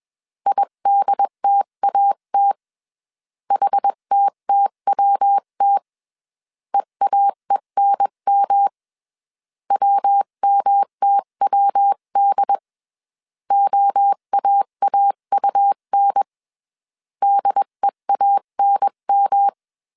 Morse code, heard as a keyed tone, about 21 wpm, 800 Hz. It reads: IDTAT HTTWT EAENM WMTWD OAAUN DEANM
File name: 20wpm